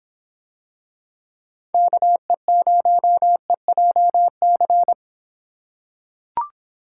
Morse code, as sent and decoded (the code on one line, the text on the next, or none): -.- . ----- . .--- -.-.
KE0EJC